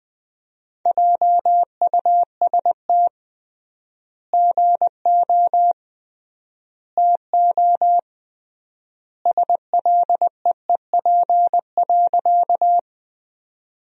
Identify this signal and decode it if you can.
Morse code, heard as a keyed tone, about 21 wpm, 700 Hz. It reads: JUST GO TO SLEEP.
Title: 3x Word-Spacing - Sentences from Top 500 Words - 20wpm